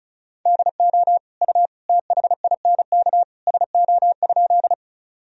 Morse code, heard as a keyed tone, about 36 wpm, 700 Hz. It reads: DO U THINK SO?
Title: Sentences from Top 200 Words and Abbreviations 35wpm